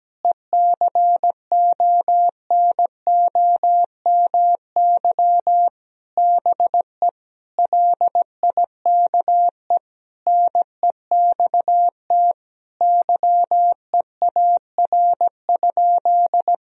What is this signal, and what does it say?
Morse code, heard as a keyed tone, about 17 wpm, 700 Hz. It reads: ECONOMY BE LIKE NEXT YEAR?
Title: Sentences from Top 1000 Words - Review Part 3 - 17wpm